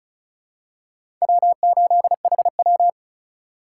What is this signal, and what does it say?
Morse code, heard as a keyed tone, about 36 wpm, 700 Hz. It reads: W8HW
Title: K1USN SST QSO Practice - 35wpm